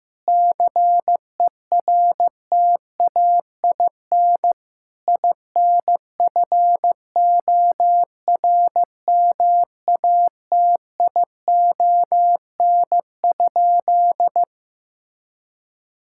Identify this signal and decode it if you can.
Morse code, heard as a keyed tone, about 15 wpm, 700 Hz. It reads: CERTAIN INFORMATION?
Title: Sentences from Top 1000 Words - Review Part 6 - 15wpm